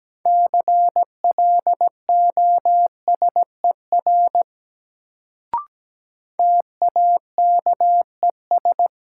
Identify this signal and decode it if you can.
Morse code, heard as a keyed tone, about 17 wpm, 700 Hz. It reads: CLOSER TAKES
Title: Sets of 5 Words Encore - New Word Drill - 17wpm